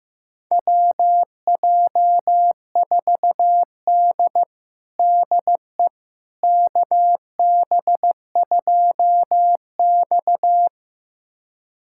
Morse code, heard as a keyed tone, about 15 wpm, 700 Hz. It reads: WJ4D DE KB2X
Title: USA Callsign DE Callsign - 4 Characters - 15wpm